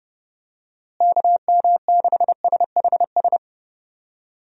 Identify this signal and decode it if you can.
Morse code, heard as a keyed tone, about 30 wpm, 700 Hz. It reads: KM6SHS
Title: USA Call Signs - 6 Characters - Repeated 2x - 30wpm